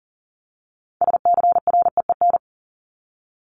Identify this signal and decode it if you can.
Morse code, heard as a keyed tone, about 40 wpm, 700 Hz. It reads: SCREEN
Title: Top 751-1000 Words 40wpm